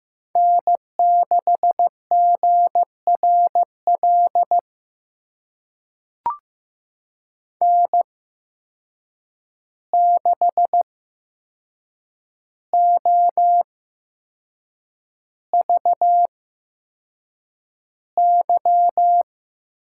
Morse code, heard as a keyed tone, about 15 wpm, spaced as for 6 wpm, 700 Hz. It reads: N6GRL N6OVY
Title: USA Call Signs - 8x Character Spacing - 15wpm